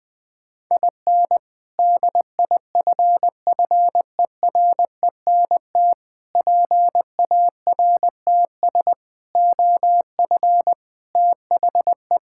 Morse code, heard as a keyed tone, about 20 wpm, 700 Hz. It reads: IN DIFFERENT PARTS OF THE
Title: Sentences with 3-Word Phrases - Spoken Beginning - 20wpm